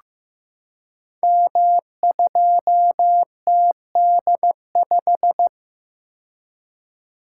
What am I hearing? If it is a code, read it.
Morse code, heard as a keyed tone, about 15 wpm, 700 Hz. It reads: M2TD5
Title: Code Groups - 5 Letter:Numbers - 15wpm